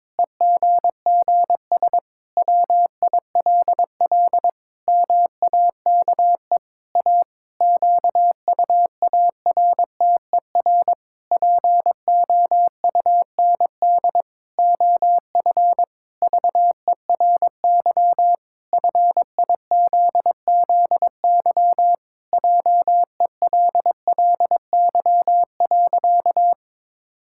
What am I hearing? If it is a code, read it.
Morse code, heard as a keyed tone, about 22 wpm, 700 Hz. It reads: EGGS WILL MAKE A QUARTER POUND OF VERY FIZZY JELLY.
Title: Pangram Challenge v2 - 22wpm